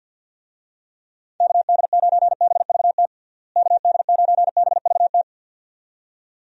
Morse code, heard as a keyed tone, about 52 wpm, 700 Hz. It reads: KD9BVT KD9BVT
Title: USA Call Signs - 6 Characters - Repeated 2x - 50wpm